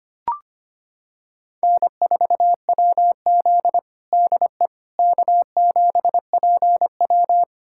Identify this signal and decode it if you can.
Morse code, heard as a keyed tone, about 25 wpm, 700 Hz. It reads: N4WZ DE K7PW
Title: USA Callsign DE Callsign - 4 Characters - 25wpm